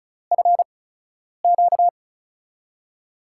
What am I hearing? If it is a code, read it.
Morse code, heard as a keyed tone, about 36 wpm, spaced as for 10 wpm, 700 Hz. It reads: FQ
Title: USA Call Signs - 8x Character Spacing - 35wpm